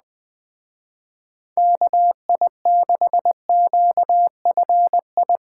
Morse code, heard as a keyed tone, about 20 wpm, 700 Hz. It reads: KI6QFI
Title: Speed Racing - USA 2x3 Call Signs 17wpm